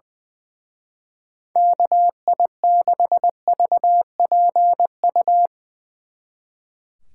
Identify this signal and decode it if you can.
Morse code, heard as a keyed tone, about 22 wpm, 700 Hz. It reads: KI6VPU
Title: Speed Racing - USA 2x3 Call Signs 20wpm